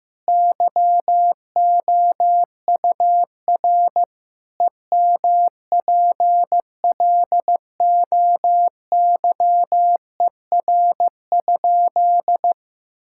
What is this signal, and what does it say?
Morse code, heard as a keyed tone, about 15 wpm, 700 Hz. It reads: YOUR EMPLOYER?
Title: Sentences from Top 1600 Words 15wpm